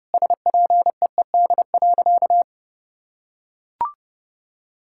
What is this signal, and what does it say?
Morse code, heard as a keyed tone, about 32 wpm, 700 Hz. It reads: SPEED.
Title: Sentences from Top 1000 Words - Review Part 5 - 30wpm